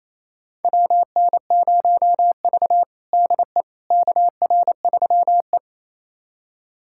Morse code, heard as a keyed tone, about 28 wpm, 700 Hz. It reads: WN0V DE KR3E
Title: USA Callsign DE Callsign - 4 Characters - 28wpm